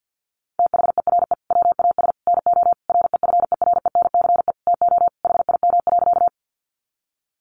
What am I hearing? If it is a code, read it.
Morse code, heard as a keyed tone, about 49 wpm, 700 Hz. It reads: THERE WAS NO REFERENCE TO HIM.
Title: Sentences from Top 1200 Words 50wpm